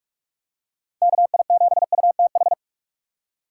Morse code, heard as a keyed tone, about 44 wpm, 700 Hz. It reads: KI7UTH
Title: USA Call Signs - 6 Characters - 2x Character Spacing - 45wpm